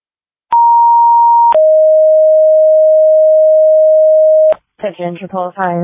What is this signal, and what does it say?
Two-tone page, heard as a one-second tone, then a three-second tone, about 940 then 620 Hz, with speech. It is heard from 0.5 to 4.5 s.